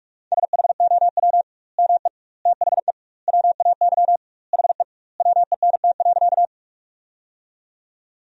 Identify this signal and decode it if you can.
Morse code, heard as a keyed tone, about 46 wpm, 700 Hz. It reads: SHOW ME THE WAY HE WENT.
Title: 2x Word-Spacing - Sentences from Top 200 Words - 45wpm